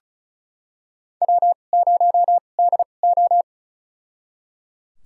Morse code, heard as a keyed tone, about 34 wpm, spaced as for 25 wpm, 700 Hz. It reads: W0DO
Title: USA Call Signs - 4 Characters - 2x Character Spacing - 35wpm